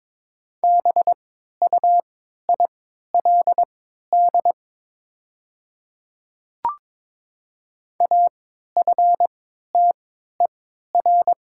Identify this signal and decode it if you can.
Morse code, heard as a keyed tone, about 22 wpm, spaced as for 12 wpm, 700 Hz. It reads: BUILD AFTER
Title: 3x Character Spacing - Top 200 Words - 22wpm